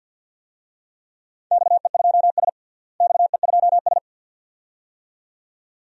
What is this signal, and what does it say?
Morse code, heard as a keyed tone, about 52 wpm, 700 Hz. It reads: XE2S XE2S
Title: DXCC-100 Call Signs - 4 Characters - Repeated 2x - 50wpm